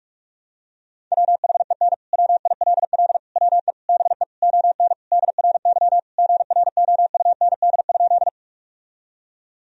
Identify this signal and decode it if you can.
Morse code, heard as a keyed tone, about 46 wpm, 700 Hz. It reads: WHEN WILL WE BE ON DRY GROUND?
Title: Sentences from Top 1000 Words - Review Part 3 - 45wpm